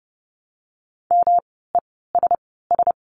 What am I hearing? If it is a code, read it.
Morse code, heard as a keyed tone, about 30 wpm, spaced as for 17 wpm, 700 Hz. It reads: MESS
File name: course / 30:10wpm / 108 Lesson 18f - Letter m - Words 30wpm